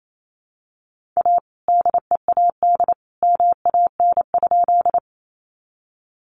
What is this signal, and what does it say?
Morse code, heard as a keyed tone, about 28 wpm, 700 Hz. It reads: A DEAD MAN?
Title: Sentences from Top 1500 Words 28wpm